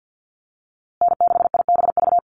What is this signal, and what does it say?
Morse code, heard as a keyed tone, about 50 wpm, 700 Hz. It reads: N6IBV
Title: USA Call Signs Repeated 2x 50wpm